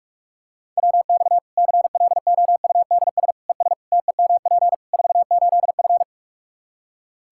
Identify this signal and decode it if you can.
Morse code, heard as a keyed tone, about 46 wpm, 700 Hz. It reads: WX CLOUDS ES TEMP 48F
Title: Practice QSO - Exchange 2 - Info - 45wpm